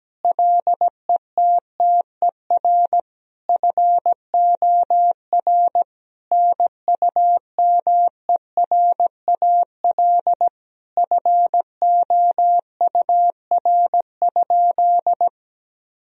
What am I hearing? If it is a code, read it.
Morse code, heard as a keyed tone, about 17 wpm, 700 Hz. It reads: LETTER FOR NUMERAL FOUR?
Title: Sentences from Top 1000 Words - Review Part 2 - 17wpm